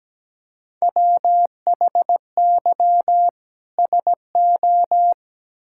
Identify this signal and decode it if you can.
Morse code, heard as a keyed tone, about 17 wpm, 700 Hz. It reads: WHY SO
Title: Sentences from Top 1000 Words - Review Part 5 - 17wpm